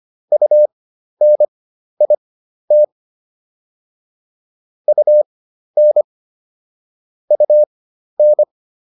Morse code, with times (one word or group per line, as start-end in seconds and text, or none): 0.3-2.8 UNIT
4.9-6.0 UN
7.3-8.4 UN